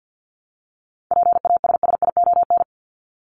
Morse code, heard as a keyed tone, about 48 wpm, 700 Hz. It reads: PASSION
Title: Top 1001-1250 Words 50wpm